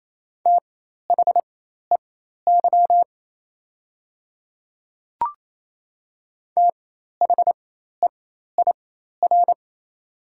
Morse code, heard as a keyed tone, about 27 wpm, spaced as for 13 wpm, 700 Hz. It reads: THEY THEIR
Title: 4x Character Spacing - Top 100 Words - 28wpm